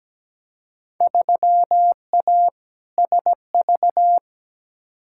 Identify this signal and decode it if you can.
Morse code, heard as a keyed tone, about 17 wpm, 700 Hz. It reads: 3A SV
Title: Field Day Exchanges - 17wpm